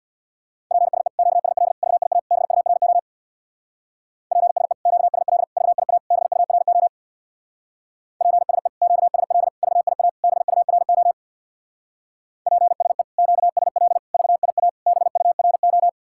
Morse code, heard as a keyed tone, about 64 wpm, 700 Hz. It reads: PSE QSL VIA BURO PSE QSL VIA BURO PSE QSL VIA BURO PSE QSL VIA BURO